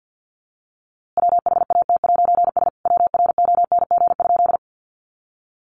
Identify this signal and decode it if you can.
Morse code, heard as a keyed tone, about 48 wpm, 700 Hz. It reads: WHAT'S WRONG?
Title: A Sound Of Thunder 50wpm